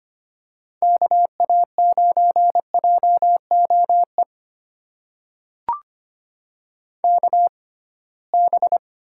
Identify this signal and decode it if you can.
Morse code, heard as a keyed tone, about 25 wpm, spaced as for 19 wpm, 700 Hz. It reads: KA9JOE KB
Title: USA Call Signs - 6x Character Spacing - 25wpm